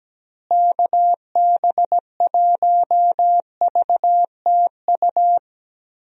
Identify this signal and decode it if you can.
Morse code, heard as a keyed tone, about 17 wpm, 700 Hz. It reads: KB1VTU
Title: USA Call Signs - 6 Characters - Repeated 3x - 17wpm